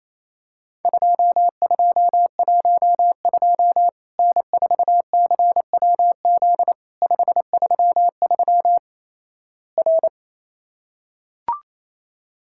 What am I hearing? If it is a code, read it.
Morse code, heard as a keyed tone, about 28 wpm, 700 Hz. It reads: 2212 N4CWZ 533 R
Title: Worked All Europe DX Contest - QTC Exchange - 28wpm